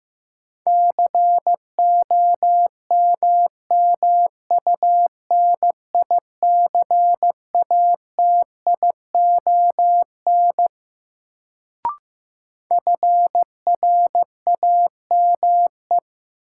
Morse code, heard as a keyed tone, about 15 wpm, 700 Hz. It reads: COMMUNICATION FRAME
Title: Sentences from Top 1000 Words - Review Part 6 - New Word Drill - 15wpm